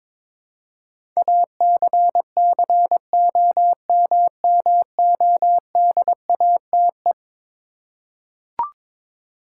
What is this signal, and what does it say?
Morse code, heard as a keyed tone, about 22 wpm, 700 Hz. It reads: ACCOMMODATE
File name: practice / 22wpm / Eleven Letter Words - 22wpm